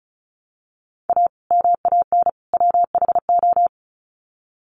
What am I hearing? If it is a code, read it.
Morse code, heard as a keyed tone, about 34 wpm, 700 Hz. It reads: A MAN WHO
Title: Sets of 3 Words 35wpm